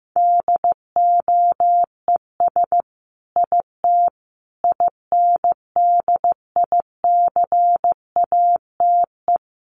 Morse code, heard as a keyed tone, about 15 wpm, 700 Hz. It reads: DOES IT INDICATE
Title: Sentences from Top 1300 Words 15wpm